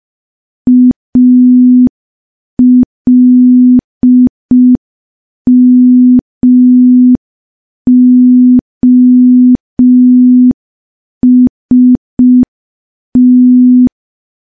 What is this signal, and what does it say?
Morse code, heard as a keyed tone, about 5 wpm, 250 Hz. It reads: ALMOST
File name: morse / morse-training-05WPM